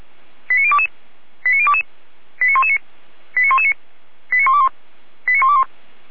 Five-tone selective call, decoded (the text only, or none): 89022 89022 89119 89119 89121 89121